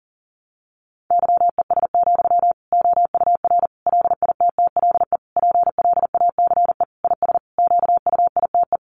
Morse code, heard as a keyed tone, about 37 wpm, 700 Hz. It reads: YES, OUR LITTLE PLACE IS QUITE
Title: Sentences from Top 700 Words 40wpm